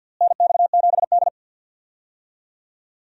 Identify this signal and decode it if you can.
Morse code, heard as a keyed tone, about 46 wpm, 700 Hz. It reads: NX7D
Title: USA Call Signs - 4 Characters - Repeated 2x - 50wpm